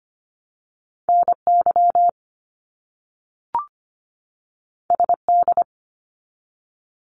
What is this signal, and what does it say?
Morse code, heard as a keyed tone, about 25 wpm, 700 Hz. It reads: NY SD
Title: US States 25wpm